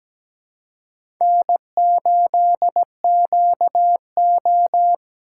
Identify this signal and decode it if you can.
Morse code, heard as a keyed tone, about 17 wpm, 700 Hz. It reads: N8QO
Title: USA Call Signs - 4 Characters - 2x Character Spacing - 17wpm